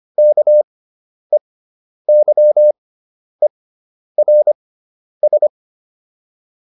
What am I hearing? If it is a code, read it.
Morse code, heard as a keyed tone, about 25 wpm, spaced as for 10 wpm, 600 Hz. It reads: KEYERS